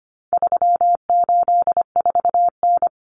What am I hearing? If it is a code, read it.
Morse code, heard as a keyed tone, about 25 wpm, 700 Hz. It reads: 384N
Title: Common Cut Numbers - 4-Digit 25wpm